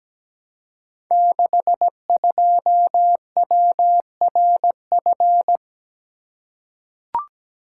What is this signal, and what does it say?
Morse code, heard as a keyed tone, about 17 wpm, 700 Hz. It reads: 62WRF
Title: Code Groups - 5 Letter:Numbers - 17wpm